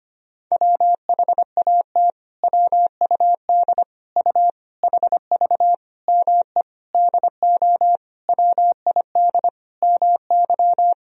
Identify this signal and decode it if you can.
Morse code, heard as a keyed tone, about 25 wpm, 700 Hz. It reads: WHAT WUD U HV ME DO WID MY
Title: Sentences from Top 300 Words and Abbreviations 25wpm